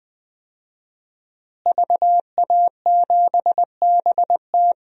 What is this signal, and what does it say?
Morse code, heard as a keyed tone, about 21 wpm, 700 Hz. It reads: VA7BT
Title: DXCC-100 Call Signs - 5 Characters - Repeated 2x - 20wpm